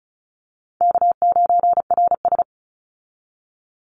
Morse code, heard as a keyed tone, about 34 wpm, 700 Hz. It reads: K9RS
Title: Easy Call Signs Repeated 3x - 35wpm